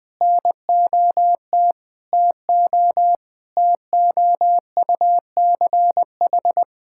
Morse code, heard as a keyed tone, about 21 wpm, 700 Hz. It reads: NOT TO TOUCH
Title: Sentences from Top 1600 Words 20wpm